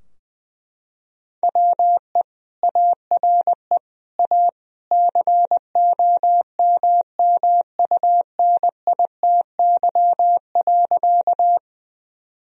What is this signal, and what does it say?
Morse code, heard as a keyed tone, about 21 wpm, 700 Hz. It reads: WE ARE A COMMUNITY.
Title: Sentences from Top 1000 Words - Review Part 4 - 20wpm